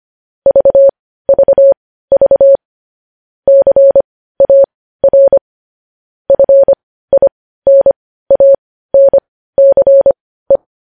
Morse code, heard as a keyed tone, about 25 wpm, spaced as for 15 wpm, 550 Hz. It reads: VVV CAR FINANCE
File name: podcast / News headlines-1x-25x15